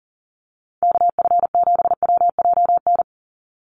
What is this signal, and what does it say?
Morse code, heard as a keyed tone, about 40 wpm, 700 Hz. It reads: KF7WJN
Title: USA Call Signs 40wpm